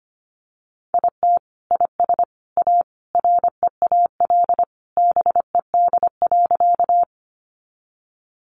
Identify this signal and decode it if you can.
Morse code, heard as a keyed tone, about 25 wpm, 700 Hz. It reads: IT IS A REAL BED.
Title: Sentences from Top 500 Words 25wpm